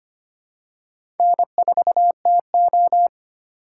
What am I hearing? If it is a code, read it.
Morse code, heard as a keyed tone, about 25 wpm, 700 Hz. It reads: N4TO
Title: USA Call Signs - 4 Characters - 8x Character Spacing - 25wpm